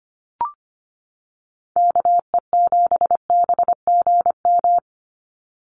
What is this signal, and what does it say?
Morse code, heard as a keyed tone, about 25 wpm, 700 Hz. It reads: KE7BGM
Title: NA SOTA Activator Callsigns 25wpm